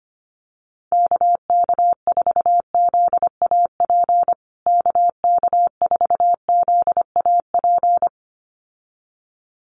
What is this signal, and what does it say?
Morse code, heard as a keyed tone, about 25 wpm, 700 Hz. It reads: KK4ZAP KK4ZAP
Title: USA Call Signs Repeated 2x 25wpm